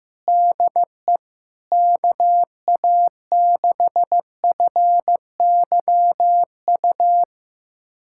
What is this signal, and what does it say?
Morse code, heard as a keyed tone, about 15 wpm, 700 Hz. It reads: DE KA6FYU
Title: USA Callsign DE Callsign - 6 Characters - 15wpm